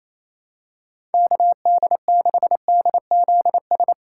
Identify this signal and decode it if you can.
Morse code, heard as a keyed tone, about 28 wpm, 700 Hz. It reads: KD6DZS
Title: USA Call Signs - 6 Characters - 6x Character Spacing - 28wpm